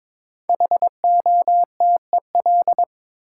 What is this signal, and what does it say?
Morse code, heard as a keyed tone, about 22 wpm, 700 Hz. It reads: HOTEL
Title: Sentences from Top 1600 Words - New Word Drill - 22wpm